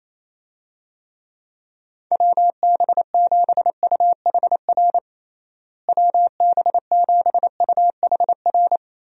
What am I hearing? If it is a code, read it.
Morse code, heard as a keyed tone, about 28 wpm, 700 Hz. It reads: WB7UHR WB7UHR
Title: USA Call Signs - 6 Characters - Repeated 3x - 28wpm